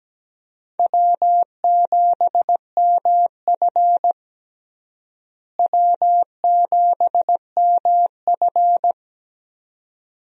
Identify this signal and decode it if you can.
Morse code, heard as a keyed tone, about 17 wpm, 700 Hz. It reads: W7MF W7MF
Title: USA Call Signs - 4 Characters - Repeated 2x - 17wpm